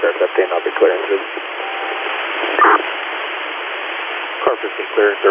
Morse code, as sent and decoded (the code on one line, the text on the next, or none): none